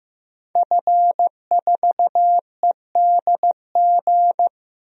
Morse code, heard as a keyed tone, about 15 wpm, 700 Hz. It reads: F4EDG
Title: DXCC-100 Call Signs - 5 Characters - 15wpm